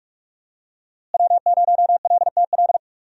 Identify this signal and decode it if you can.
Morse code, heard as a keyed tone, about 46 wpm, 700 Hz. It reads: W0LTL
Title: USA Call Signs - 5 Characters - Repeated 3x - 45wpm